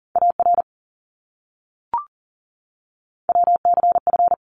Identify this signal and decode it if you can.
Morse code, heard as a keyed tone, about 40 wpm, 700 Hz. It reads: AR WCF
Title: ARRL Sections 40wpm